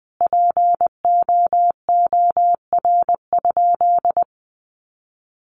Morse code, heard as a keyed tone, about 20 wpm, 700 Hz. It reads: POOR?
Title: Sentences from Top 1100 Words 20wpm